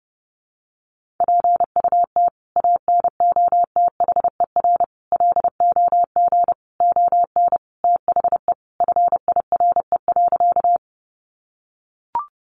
Morse code, heard as a keyed tone, about 30 wpm, 700 Hz. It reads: PUT ANOTHER LOG ON THE FIRE.
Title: Sentences from Top 1400 Words 30wpm